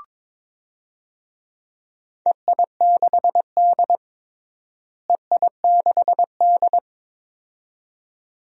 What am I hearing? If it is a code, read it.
Morse code, heard as a keyed tone, about 22 wpm, 700 Hz. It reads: EI6D EI6D
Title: DXCC-100 Call Signs - 4 Characters - Repeated 2x - 22wpm